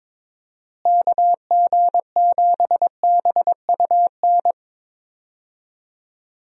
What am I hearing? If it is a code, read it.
Morse code, heard as a keyed tone, about 22 wpm, 700 Hz. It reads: KG7BUN
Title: USA Call Signs - 6 Characters - Repeated 2x - 22wpm